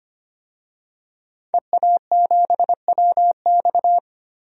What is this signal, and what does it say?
Morse code, heard as a keyed tone, about 25 wpm, 700 Hz. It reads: EA7WX
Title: DXCC-100 Call Signs - 5 Characters - 25wpm